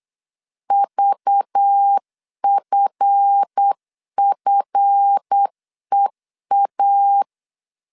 Morse code, heard as a keyed tone, about 8 wpm, 800 Hz. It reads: VFFEA
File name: LEZIONI / 7 GF9Z / 7 GF 8wpm 3108